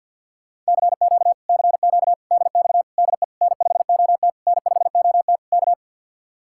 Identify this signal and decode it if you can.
Morse code, heard as a keyed tone, about 61 wpm, 700 Hz. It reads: CQ CQ DX DE N5OT N5OT K
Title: Speed-Racing Common QSO Phrases - 50wpm